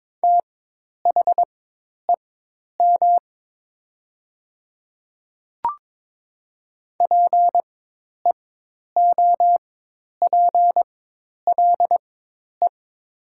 Morse code, heard as a keyed tone, about 22 wpm, spaced as for 10 wpm, 700 Hz. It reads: THEM PEOPLE
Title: 4x Character Spacing - Top 200 Words - 22wpm